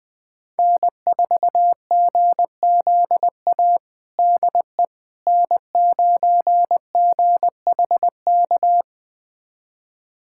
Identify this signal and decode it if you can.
Morse code, heard as a keyed tone, about 20 wpm, 700 Hz. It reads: N4GZA DE N9GHK
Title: USA Callsign DE Callsign - 5 Characters - 20wpm